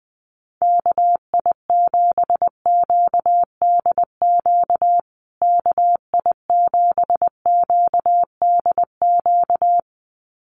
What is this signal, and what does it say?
Morse code, heard as a keyed tone, about 20 wpm, 700 Hz. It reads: KI7QDQ KI7QDQ
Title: USA Call Signs Repeated 2x 20wpm